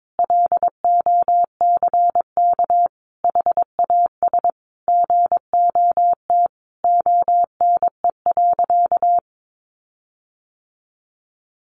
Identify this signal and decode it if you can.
Morse code, heard as a keyed tone, about 22 wpm, 700 Hz. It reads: LOCK HAS GOT ONE.
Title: Sentences from Top 1300 Words 22wpm